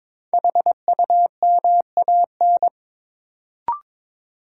Morse code, heard as a keyed tone, about 22 wpm, 700 Hz. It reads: HUMAN
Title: Sentences from Top 600 Words - New Word Drill - 22wpm